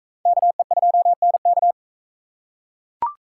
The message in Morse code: KE2NK

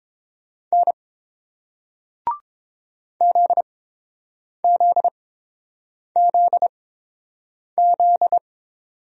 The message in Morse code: N ZZZZ

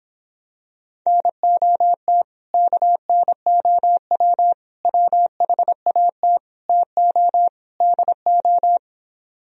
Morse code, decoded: NOT KNOW WHAT TO DO

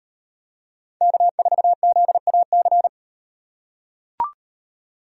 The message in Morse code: K4ZAC

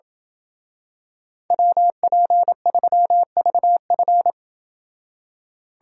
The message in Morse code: WP3VF

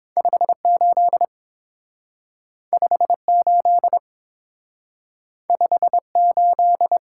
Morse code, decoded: 58 58 58

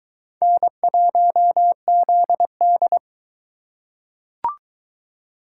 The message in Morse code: N1ZD

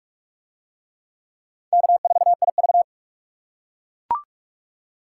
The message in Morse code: K4IV

